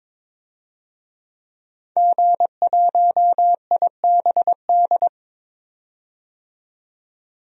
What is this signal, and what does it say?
Morse code, heard as a keyed tone, about 22 wpm, 700 Hz. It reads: G1IBD
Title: DXCC-100 Call Signs - 5 Characters - 22wpm